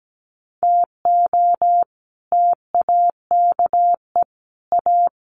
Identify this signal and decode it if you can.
Morse code, heard as a keyed tone, about 17 wpm, 700 Hz. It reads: TO TAKE A
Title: Sets of 5 Words Encore2 17wpm